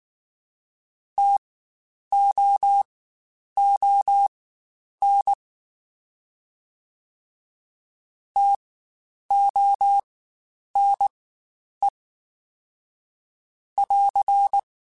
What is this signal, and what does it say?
Morse code, heard as a keyed tone, about 19 wpm, spaced as for 9 wpm, 800 Hz. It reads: TOON TONE <AR>